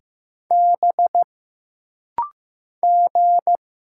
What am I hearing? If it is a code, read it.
Morse code, heard as a keyed tone, about 15 wpm, 700 Hz. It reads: B G
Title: Single Letter-Number Alt - Rapid-Fire 15wpm